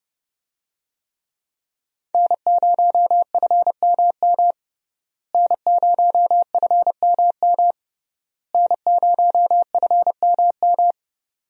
Morse code, heard as a keyed tone, about 30 wpm, 700 Hz. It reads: N0FMM N0FMM N0FMM
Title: USA Call Signs - 5 Characters - Repeated 3x - 30wpm